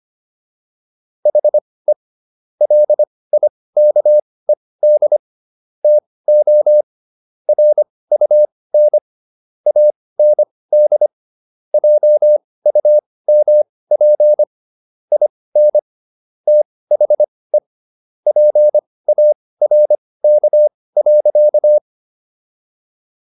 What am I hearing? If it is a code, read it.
Morse code, heard as a keyed tone, about 25 wpm, spaced as for 18 wpm, 600 Hz. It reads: HE LIKED TO RUN AND JUMP IN THE PARK.